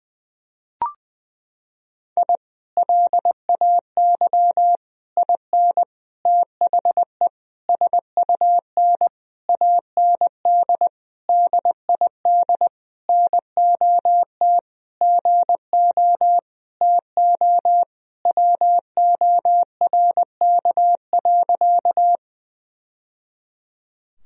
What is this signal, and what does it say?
Morse code, heard as a keyed tone, about 19 wpm, 700 Hz. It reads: I LAY IN THE SUN AND DID NOT GO TO WORK.
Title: Sentences from Top 1000 Words - Review Part 2 - 20wpm